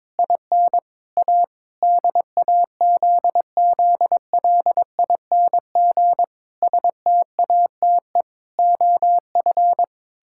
Morse code, heard as a keyed tone, about 22 wpm, 700 Hz. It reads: IN A DAZZLING STATE OF
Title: Pangram Challenge v2 - 22wpm